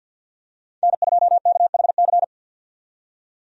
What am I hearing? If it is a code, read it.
Morse code, heard as a keyed tone, about 50 wpm, 700 Hz. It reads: N2KHC
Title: USA Call Signs - 5 Characters - 50wpm